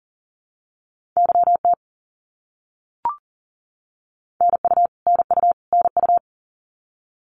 Morse code, heard as a keyed tone, about 40 wpm, 700 Hz. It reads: YT NU NU NU
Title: Canadian Provinces and Territories - Repeated 3x - 40wpm